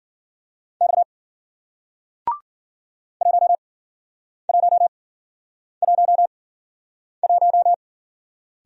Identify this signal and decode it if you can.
Morse code, heard as a keyed tone, about 50 wpm, spaced as for 9 wpm, 700 Hz. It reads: X 1111